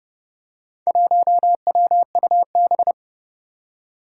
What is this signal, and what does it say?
Morse code, heard as a keyed tone, about 32 wpm, 700 Hz. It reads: 1WUB
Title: Four Challenging Letter-Number Combinations 30wpm